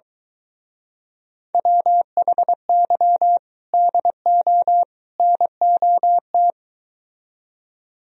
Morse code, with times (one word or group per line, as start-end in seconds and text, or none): 1.5-3.4 WHY
3.7-4.8 DO
5.2-6.5 NOT